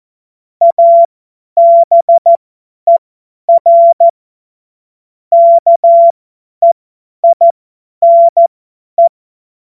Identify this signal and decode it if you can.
Morse code, heard as a keyed tone, about 14 wpm, spaced as for 10 wpm, 680 Hz. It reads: ABER KEINE
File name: Morsekurs / Witze / Witz 14. Februar 2023 50 F70